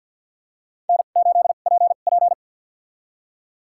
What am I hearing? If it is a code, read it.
Morse code, heard as a keyed tone, about 50 wpm, spaced as for 36 wpm, 700 Hz. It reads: N8PP
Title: USA Call Signs - 2x Character Spacing - 50wpm